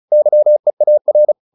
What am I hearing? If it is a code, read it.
Morse code, heard as a keyed tone, about 34 wpm, 600 Hz. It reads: YEAR
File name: PR401 35